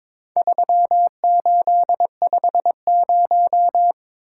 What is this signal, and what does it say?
Morse code, heard as a keyed tone, about 22 wpm, 700 Hz. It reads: 3850
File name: practice / 22wpm / Four Letter-Number Combinations - 22wpm